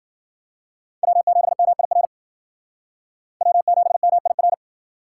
Morse code, heard as a keyed tone, about 52 wpm, 700 Hz. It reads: W7MIR W7MIR